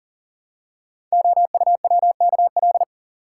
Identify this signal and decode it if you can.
Morse code, heard as a keyed tone, about 38 wpm, 690 Hz. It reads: OUWKL
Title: Code Groups - 5 Letters - 40wpm